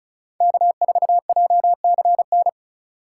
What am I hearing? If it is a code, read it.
Morse code, heard as a keyed tone, about 34 wpm, 700 Hz. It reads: K4JCN
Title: USA Call Signs - 5 Characters - 4x Character Spacing - 35wpm